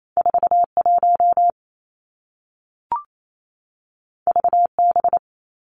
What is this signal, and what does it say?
Morse code, heard as a keyed tone, about 28 wpm, 700 Hz. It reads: 4 1 V B
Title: Two Challenging Letter-Number Combinations 28wpm